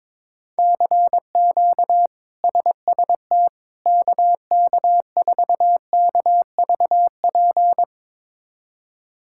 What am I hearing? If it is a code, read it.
Morse code, heard as a keyed tone, about 22 wpm, 700 Hz. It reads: CQ SST KK4KVP
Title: K1USN SST QSO Practice - 22wpm